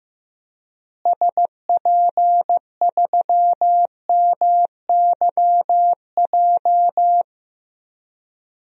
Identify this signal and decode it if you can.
Morse code, heard as a keyed tone, about 15 wpm, 700 Hz. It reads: SP3MYJ
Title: DXCC-100 Call Signs - 6 Characters - Repeated 3x - 15wpm